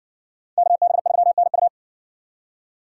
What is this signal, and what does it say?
Morse code, heard as a keyed tone, about 60 wpm, 700 Hz. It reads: KB3NU